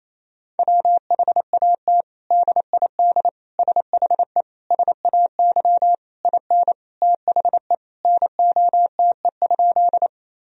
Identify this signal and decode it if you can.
Morse code, heard as a keyed tone, about 27 wpm, 700 Hz. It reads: WHAT DID SHE SAY IN THE NOTE?
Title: Sentences from Top 1000 Words - Review Part 2 - 28wpm